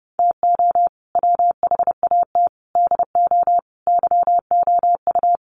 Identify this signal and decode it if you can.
Morse code, heard as a keyed tone, about 30 wpm, 700 Hz. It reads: TO WHAT DO YOU
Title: Sentences from Top 1000 Words 30wpm